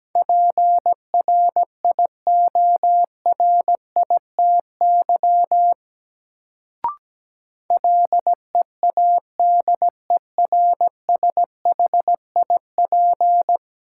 Morse code, heard as a keyed tone, about 17 wpm, 700 Hz. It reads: PRIORITY LEADERSHIP
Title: Sentences from Top 1600 Words - New Word Drill Alt - 17wpm